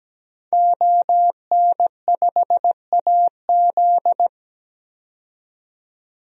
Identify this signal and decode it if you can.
Morse code, heard as a keyed tone, about 17 wpm, 700 Hz. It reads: ON5AZ
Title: DXCC-100 Call Signs - 5 Characters - 17wpm